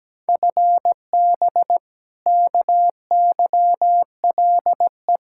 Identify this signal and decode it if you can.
Morse code, heard as a keyed tone, about 17 wpm, 700 Hz. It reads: FB KYLE
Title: Practice QSO - Exchange 3 - More Info - 17wpm